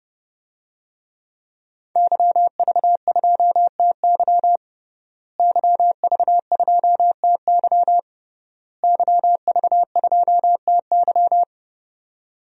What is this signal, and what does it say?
Morse code, heard as a keyed tone, about 30 wpm, 700 Hz. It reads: YV2TY YV2TY YV2TY
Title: DXCC-100 Call Signs - 5 Characters - Repeated 3x - 30wpm